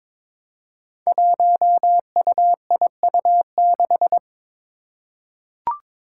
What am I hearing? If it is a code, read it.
Morse code, heard as a keyed tone, about 22 wpm, 700 Hz. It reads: 1UIU6
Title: Five Challenging Letter-Number Combinations 22wpm